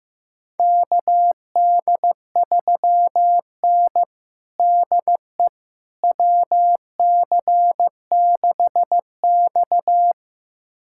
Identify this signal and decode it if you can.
Morse code, heard as a keyed tone, about 15 wpm, 700 Hz. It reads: KD3N DE WC6X
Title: USA Callsign DE Callsign - 4 Characters - 15wpm